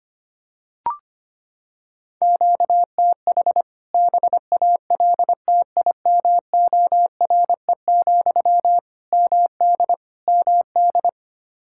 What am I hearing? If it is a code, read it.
Morse code, heard as a keyed tone, about 25 wpm, 700 Hz. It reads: QTH BALTIMORE, MD MD
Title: K1USN SST QSO Practice 25wpm